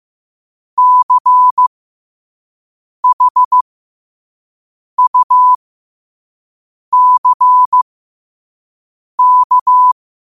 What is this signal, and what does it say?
Morse code, heard as a keyed tone, about 15 wpm, spaced as for 5 wpm, 1000 Hz. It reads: CHUCK
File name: 5wpm